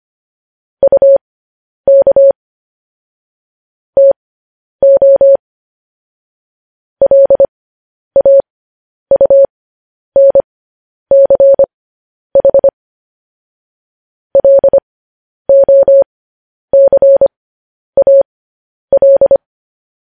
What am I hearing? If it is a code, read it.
Morse code, heard as a keyed tone, about 25 wpm, spaced as for 10 wpm, 550 Hz. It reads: UK TO LAUNCH LOCAL
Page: news-headlines-in-morse-code-at-10-wpm